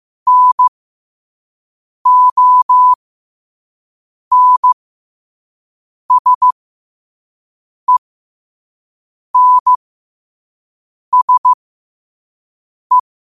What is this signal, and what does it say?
Morse code, heard as a keyed tone, about 15 wpm, spaced as for 5 wpm, 1000 Hz. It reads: NONSENSE
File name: qotdfiles / 5wpm